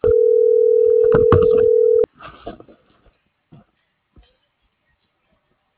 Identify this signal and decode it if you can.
Ringback tone (North America), 440 + 480 Hz.